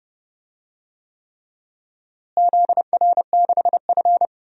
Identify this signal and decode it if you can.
Morse code, heard as a keyed tone, about 32 wpm, 700 Hz. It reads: ZR6F
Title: DXCC-100 Call Signs - 4 Characters - Repeated 3x - 30wpm